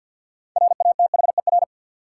Morse code, heard as a keyed tone, about 48 wpm, 700 Hz. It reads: RATHER
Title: Sentences from Top 700 Words - New Word Drill - 50wpm